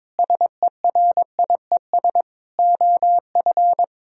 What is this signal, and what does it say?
Morse code, heard as a keyed tone, about 22 wpm, 700 Hz. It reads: SERIES OF
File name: practice / 22wpm / Sentences with 8-Word Phrases - 22wpm